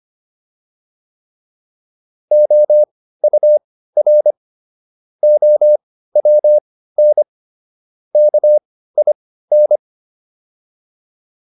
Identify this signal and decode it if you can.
Morse code, heard as a keyed tone, about 25 wpm, spaced as for 15 wpm, 600 Hz. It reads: OUR OWN KIN